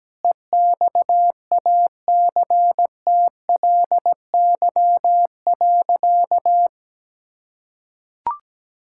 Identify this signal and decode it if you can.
Morse code, heard as a keyed tone, about 17 wpm, 700 Hz. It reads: EXACTLY.
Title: Sentences from Top 1000 Words - Review Part 6 - 17wpm